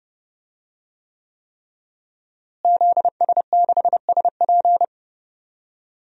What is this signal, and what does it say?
Morse code, heard as a keyed tone, about 32 wpm, 700 Hz. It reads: ZS6SP
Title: DXCC-100 Call Signs - 5 Characters - Repeated 3x - 30wpm